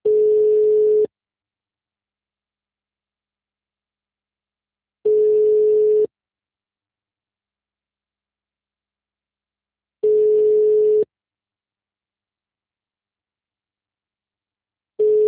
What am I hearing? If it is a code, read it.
Ringback tone (Europe, 425 Hz).